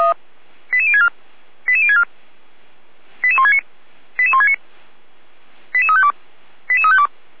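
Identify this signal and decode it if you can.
Five-tone selective call (ZVEI), heard as quick runs of five tones, tones of about 70 ms. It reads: 88064 88064 88169 88169 88352 88352
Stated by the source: ZVEI five-tone selective call received by SDR.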